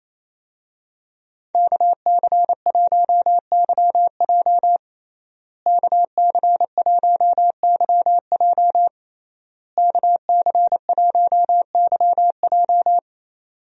Morse code, heard as a keyed tone, about 28 wpm, 700 Hz. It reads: KC1YJ KC1YJ KC1YJ
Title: USA Call Signs - 5 Characters - Repeated 3x - 28wpm